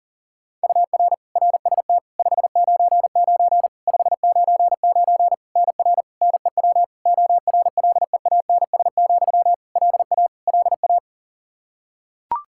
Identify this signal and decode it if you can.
Morse code, heard as a keyed tone, about 40 wpm, 700 Hz. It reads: UR RST 599 599 NR NEW ORLEANS, LA LA
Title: Practice QSO - Exchange 1 - Beginning - 40wpm